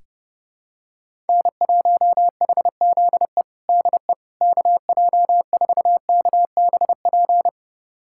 Morse code, heard as a keyed tone, about 30 wpm, 700 Hz. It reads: N1HZE DE KJ4KBP
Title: USA Callsign DE Callsign 30wpm